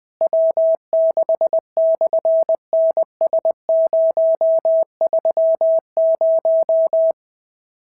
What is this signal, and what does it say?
Morse code, heard as a keyed tone, about 20 wpm, 650 Hz. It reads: W6/NS030
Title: SOTA QSO Practice 20wpm